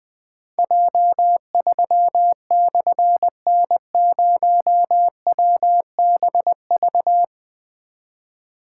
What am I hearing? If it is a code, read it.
Morse code, heard as a keyed tone, about 20 wpm, 700 Hz. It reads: J3/N0WBV
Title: Difficult Call Signs 20wpm